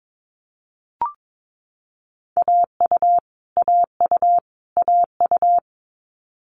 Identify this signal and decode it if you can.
Morse code, heard as a keyed tone, about 22 wpm, 700 Hz. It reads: AU AU AU
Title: Multi-letter Phonemes Repeated 3x 22wpm